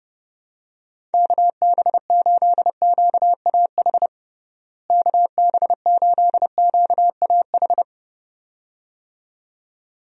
Morse code, heard as a keyed tone, about 30 wpm, 700 Hz. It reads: KB8QAH KB8QAH
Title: USA Call Signs - 6 Characters - Repeated 2x - 30wpm